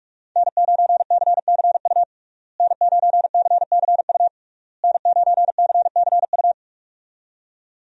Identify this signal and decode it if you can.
Morse code, heard as a keyed tone, about 41 wpm, 700 Hz. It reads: N9CCU N9CCU N9CCU
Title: USA Call Signs - 5 Characters - Repeated 3x - 45wpm